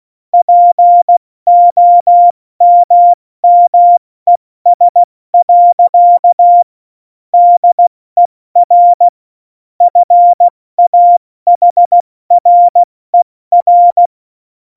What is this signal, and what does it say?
Morse code, heard as a keyed tone, about 16 wpm, 700 Hz. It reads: POMMES. DER FAHRER
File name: Morsekurs / Witze / Witz 12. Februar 2019, 70 F80